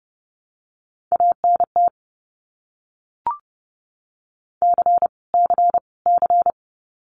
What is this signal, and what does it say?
Morse code, heard as a keyed tone, about 30 wpm, 700 Hz. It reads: ANT C C C
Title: Abbreviations Repeated 3x - 30wpm